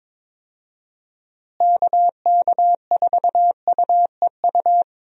Morse code, heard as a keyed tone, about 22 wpm, 700 Hz. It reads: KK4UEU
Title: USA Call Signs - 6 Characters - Repeated 2x - 22wpm